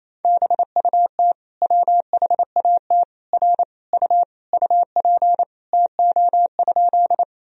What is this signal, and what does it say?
Morse code, heard as a keyed tone, about 28 wpm, 700 Hz. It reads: BUT WHAT R U UP TO?
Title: Sentences from Top 100 Words and Abbreviations 28wpm